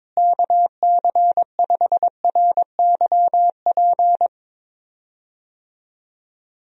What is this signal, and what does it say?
Morse code, heard as a keyed tone, about 22 wpm, 700 Hz. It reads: KC5RYP
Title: USA Call Signs - 6 Characters - Repeated 2x - 22wpm